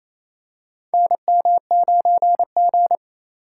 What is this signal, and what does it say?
Morse code, heard as a keyed tone, about 28 wpm, 700 Hz. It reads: NM9G
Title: USA Call Signs - 4 Characters - 4x Character Spacing - 28wpm